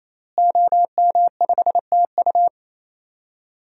Morse code, heard as a keyed tone, about 27 wpm, 700 Hz. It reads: OM5TU
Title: DXCC-100 Call Signs - 5 Characters - Repeated 3x - 28wpm